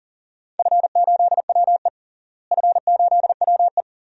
Morse code, heard as a keyed tone, about 40 wpm, 700 Hz. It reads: F8WE F8WE
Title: DXCC-100 Call Signs - 4 Characters - Repeated 2x - 40wpm